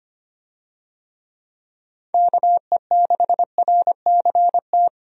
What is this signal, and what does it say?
Morse code, heard as a keyed tone, about 25 wpm, 700 Hz. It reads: KE6RCT
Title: USA Call Signs - 6 Characters - 25wpm